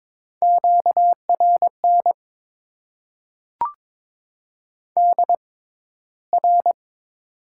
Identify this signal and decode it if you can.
Morse code, heard as a keyed tone, about 22 wpm, spaced as for 14 wpm, 700 Hz. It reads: QRN DR
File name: Top 90 QSO Elements - 6x Character Spacing - 22wpm